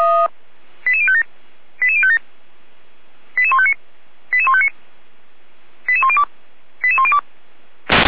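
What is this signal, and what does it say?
Five-tone selective call (ZVEI), heard as quick runs of five tones, tones of about 70 ms. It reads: 88057 88057 88159 88159 88102 88102
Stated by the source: ZVEI five-tone selective call received on SDR.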